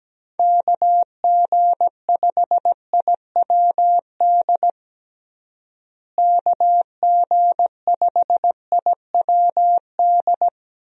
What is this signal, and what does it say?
Morse code, heard as a keyed tone, about 17 wpm, 700 Hz. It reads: KG5IWD KG5IWD
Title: USA Call Signs - 6 Characters - Repeated 2x - 17wpm